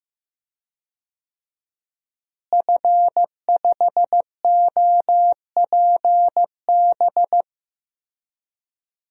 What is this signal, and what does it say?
Morse code, heard as a keyed tone, about 15 wpm, 700 Hz. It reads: F5OPB
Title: DXCC-100 Call Signs - 5 Characters - Repeated 2x - 15wpm